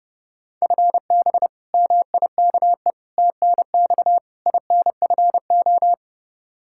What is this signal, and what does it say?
Morse code, heard as a keyed tone, about 30 wpm, 700 Hz. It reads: FB MIKE TNX INFO
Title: Practice QSO - Exchange 3 - More Info - 30wpm